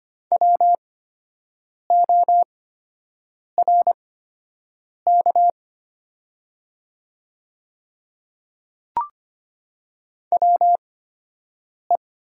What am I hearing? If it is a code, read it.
Morse code, heard as a keyed tone, about 25 wpm, spaced as for 7 wpm, 700 Hz. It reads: WORK WE
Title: 8x Character Spacing - Top 300 Words - 25wpm